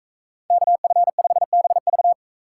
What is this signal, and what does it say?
Morse code, heard as a keyed tone, about 43 wpm, 700 Hz. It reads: KF5BV